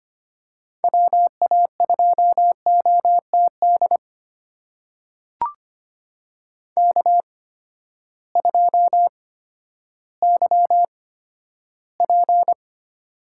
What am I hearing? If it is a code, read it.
Morse code, heard as a keyed tone, about 25 wpm, spaced as for 12 wpm, 700 Hz. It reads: WA2OTD K2YP
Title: USA Call Signs - 8x Character Spacing - 25wpm